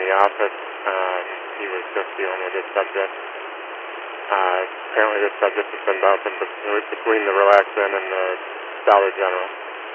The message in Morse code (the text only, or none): none